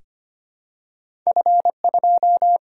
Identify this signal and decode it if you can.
Morse code, heard as a keyed tone, about 25 wpm, 700 Hz. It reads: F2
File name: Two Letter-Number Combinations - 25wpm